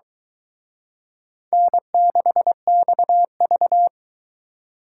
Morse code, heard as a keyed tone, about 23 wpm, 700 Hz. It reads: N6XV